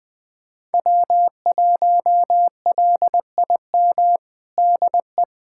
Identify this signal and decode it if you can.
Morse code, heard as a keyed tone, about 20 wpm, 700 Hz. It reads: W1LIM DE
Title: USA Callsign DE Callsign - 5 Characters - 20wpm